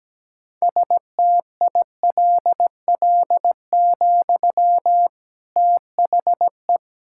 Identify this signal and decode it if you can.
Morse code, heard as a keyed tone, about 17 wpm, 700 Hz. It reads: STILL, THE
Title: Sentences from Top 1600 Words 17wpm